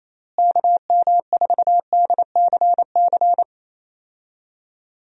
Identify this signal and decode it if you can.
Morse code, heard as a keyed tone, about 28 wpm, 700 Hz. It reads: KM4DCC
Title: USA Call Signs - 6 Characters - Repeated 3x - 28wpm